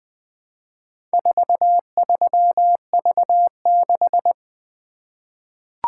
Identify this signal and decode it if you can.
Morse code, heard as a keyed tone, about 20 wpm, 700 Hz. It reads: 43V6
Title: Four Letter-Number Combinations - 20wpm